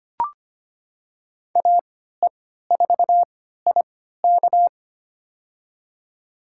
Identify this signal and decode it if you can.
Morse code, heard as a keyed tone, about 25 wpm, spaced as for 14 wpm, 700 Hz. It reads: AE4IK
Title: USA Call Signs - 5 Characters - 3x Character Spacing - 25wpm